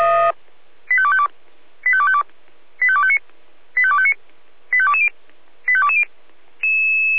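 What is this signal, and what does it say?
Five-tone selective call (ZVEI), heard as quick runs of five tones, tones of about 70 ms. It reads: 85252 85252 85259 85259 85220 85220
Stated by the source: ZVEI five-tone selective call received by SDR.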